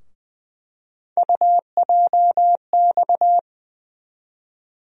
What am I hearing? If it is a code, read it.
Morse code, heard as a keyed tone, about 20 wpm, 700 Hz. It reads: UJX
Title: USA Call Sign Postfixes - 3 Letters 20wpm